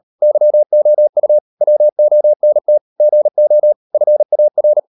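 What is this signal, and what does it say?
Morse code, heard as a keyed tone, about 38 wpm, 600 Hz. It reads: YOU WONT GO FAR
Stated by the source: Morse code practice at 38 WPM, sending Ken Follett's novel Never in English, from CW med Gnister.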